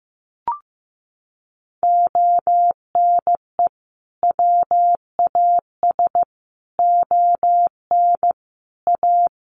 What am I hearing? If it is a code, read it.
Morse code, heard as a keyed tone, about 15 wpm, 700 Hz. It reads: ONE WAS ON A
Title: Sentences from Top 300 Words 15wpm